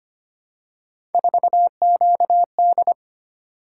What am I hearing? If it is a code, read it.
Morse code, heard as a keyed tone, about 25 wpm, 700 Hz. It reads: 4QD
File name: Three Letter-Number Combinations - 25wpm